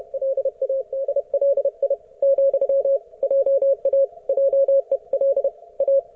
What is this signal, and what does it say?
Morse code, heard as a keyed tone, about 31 wpm, 550 Hz. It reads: LADLI, JA JEL A
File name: Archiv / CUC138 / ok1fhi cul138